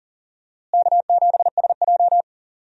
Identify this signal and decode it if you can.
Morse code, heard as a keyed tone, about 40 wpm, 700 Hz. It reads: K7SJ
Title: USA Call Signs - 4 Characters - Repeated 2x - 40wpm